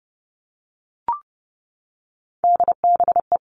Morse code, heard as a keyed tone, about 30 wpm, 700 Hz. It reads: DBE
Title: All Cut Numbers - 3-Digit 30wpm